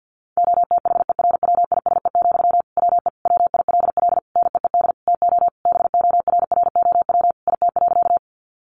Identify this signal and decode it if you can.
Morse code, heard as a keyed tone, about 50 wpm, 700 Hz. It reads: OTHERWISE, WE WILL NEED TO BORROW IT.
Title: Sentences from Top 1500 Words 50wpm